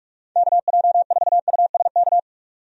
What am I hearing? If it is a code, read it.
Morse code, heard as a keyed tone, about 47 wpm, 700 Hz. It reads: KJ4USK
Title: USA Call Signs - 6 Characters - 45wpm